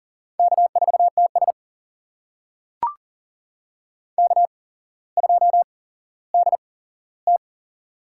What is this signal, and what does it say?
Morse code, heard as a keyed tone, about 40 wpm, spaced as for 17 wpm, 700 Hz. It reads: K4TS K2DT
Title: USA Call Signs - 4 Characters - 8x Character Spacing - 40wpm